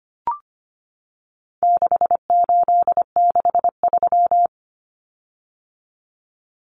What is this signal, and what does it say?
Morse code, heard as a keyed tone, about 25 wpm, 700 Hz. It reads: 6863
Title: Common Cut Numbers - 4-Digit 25wpm